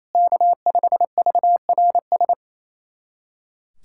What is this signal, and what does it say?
Morse code, heard as a keyed tone, about 28 wpm, 700 Hz. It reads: K5VRS
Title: SOTA QSO Practice 28wpm